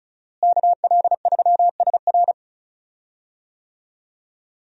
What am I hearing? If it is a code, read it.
Morse code, heard as a keyed tone, about 36 wpm, 700 Hz. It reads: KL3SR
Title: USA Call Signs - 5 Characters - Repeated 3x - 35wpm